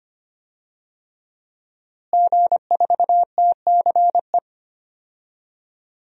Morse code, heard as a keyed tone, about 25 wpm, 700 Hz. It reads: G4TCE
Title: DXCC-100 Call Signs - 5 Characters - Repeated 3x - 25wpm